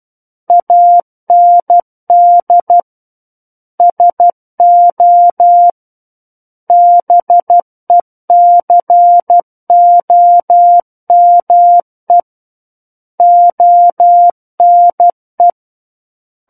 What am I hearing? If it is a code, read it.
Morse code, heard as a keyed tone, about 12 wpm, 700 Hz. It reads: AND SO BECOME ONE
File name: Book One / 03-Chapter 3 - 12 WPM